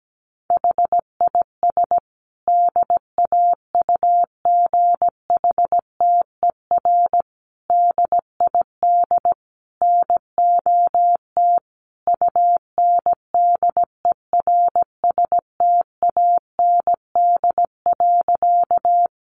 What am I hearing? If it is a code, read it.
Morse code, heard as a keyed tone, about 17 wpm, 700 Hz. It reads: HIS DAUGHTER DID NOT UNDERSTAND.
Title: Sentences from Top 1500 Words 17wpm